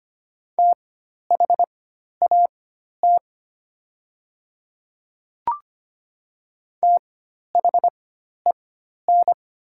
Morse code, heard as a keyed tone, about 25 wpm, spaced as for 12 wpm, 700 Hz. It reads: THAT THEN
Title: 4x Character Spacing - Top 200 Words - 25wpm